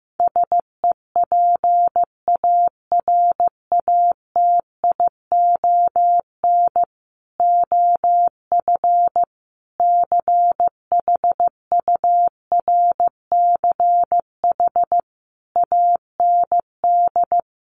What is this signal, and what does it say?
Morse code, heard as a keyed tone, about 15 wpm, 700 Hz. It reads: SEPARATION OF CHURCH AND
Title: Sets of 5 Words 15wpm